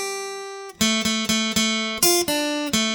There is no code.